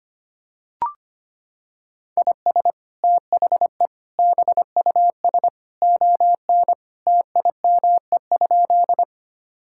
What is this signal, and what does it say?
Morse code, heard as a keyed tone, about 25 wpm, 700 Hz. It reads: IS THE BUS ON TIME?
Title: Sentences from Top 1000 Words - Review Part 6 - 25wpm